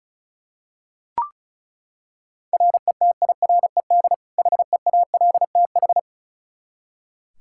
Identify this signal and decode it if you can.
Morse code, heard as a keyed tone, about 36 wpm, 700 Hz. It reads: RETIRED HEALTH
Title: Practice QSO - Exchange 3 - More Info - 35wpm